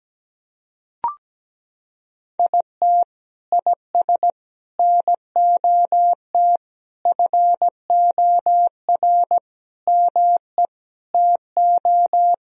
Morse code, heard as a keyed tone, about 17 wpm, 700 Hz. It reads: IT IS NOT FOR ME TO
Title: Sentences from Top 1000 Words - Review Part 1 - 17wpm